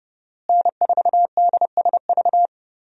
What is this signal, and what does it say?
Morse code, heard as a keyed tone, about 32 wpm, 700 Hz. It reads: N4DSV
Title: USA Call Signs - 5 Characters - Repeated 2x - 30wpm